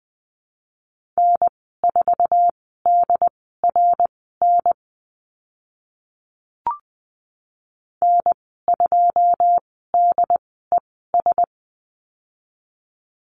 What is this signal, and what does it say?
Morse code, heard as a keyed tone, about 20 wpm, spaced as for 14 wpm, 700 Hz. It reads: N4DRN N2DES
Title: 088 Lesson 15g - Number 5 - Call Signs 20wpm